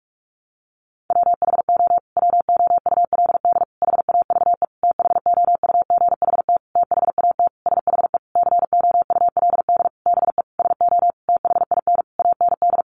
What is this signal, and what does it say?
Morse code, heard as a keyed tone, about 44 wpm, 700 Hz. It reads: WHO WOULD HAVE THOUGHT THAT SHE COULD BE SO THIN AND
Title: Sentences from Top 1200 Words 45wpm